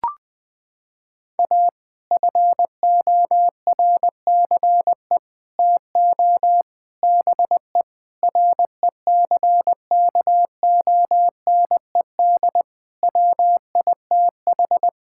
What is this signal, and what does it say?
Morse code, heard as a keyed tone, about 20 wpm, 700 Hz. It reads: A FORCE TO BE RECKONED WITH
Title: Sets of 8 Words - 20wpm